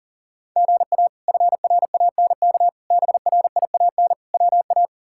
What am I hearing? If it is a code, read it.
Morse code, heard as a keyed tone, about 40 wpm, 700 Hz. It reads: GA FRANK BRIAN WA
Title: K1USN SST QSO Practice - 40wpm